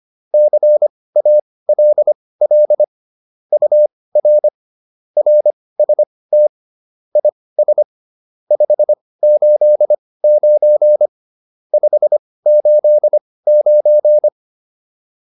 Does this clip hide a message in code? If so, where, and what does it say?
Yes: Morse code from 0.3 to 14.3 s, heard as a keyed tone, about 25 wpm, spaced as for 18 wpm, 600 Hz. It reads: CALL UR RST IS 589 589